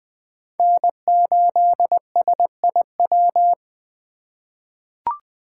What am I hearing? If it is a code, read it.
Morse code, heard as a keyed tone, about 21 wpm, 700 Hz. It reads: N8SIW
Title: USA Call Signs - 5 Characters - 4x Character Spacing - 20wpm